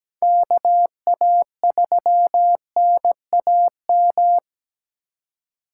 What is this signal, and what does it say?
Morse code, heard as a keyed tone, about 17 wpm, 700 Hz. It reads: KA3NAM
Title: K1USN SST QSO Practice - 17wpm